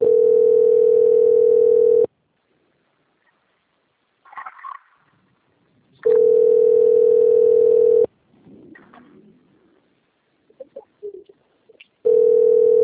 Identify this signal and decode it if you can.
Ringback tone (North America), 440 + 480 Hz.